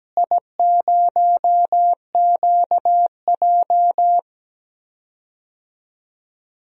Morse code, heard as a keyed tone, about 17 wpm, 700 Hz. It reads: I0QJ
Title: DXCC-100 Call Signs - 4 Characters - Repeated 2x - 17wpm